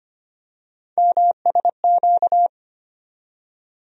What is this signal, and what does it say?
Morse code, heard as a keyed tone, about 25 wpm, 700 Hz. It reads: MSQ
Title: USA Call Sign Postfixes - 3 Letters 25wpm